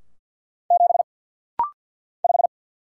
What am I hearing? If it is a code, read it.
Morse code, heard as a keyed tone, about 49 wpm, 700 Hz. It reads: 7 5
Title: Single Letter-Number - Rapid-Fire 50wpm